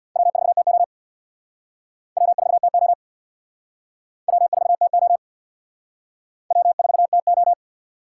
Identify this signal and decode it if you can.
Morse code, heard as a keyed tone, about 60 wpm, 700 Hz. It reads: W4TO W4TO W4TO W4TO
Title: Speed Racing - USA 1x2 Call Signs 50wpm